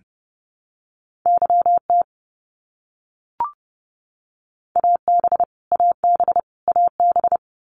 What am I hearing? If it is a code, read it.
Morse code, heard as a keyed tone, about 30 wpm, 700 Hz. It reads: YT AB AB AB
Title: Canadian Provinces and Territories - Repeated 3x - 30wpm